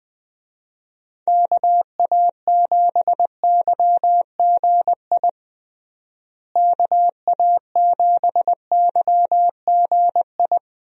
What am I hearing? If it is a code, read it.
Morse code, heard as a keyed tone, about 21 wpm, 700 Hz. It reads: KA7YGI KA7YGI
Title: USA Call Signs - 6 Characters - Repeated 3x - 20wpm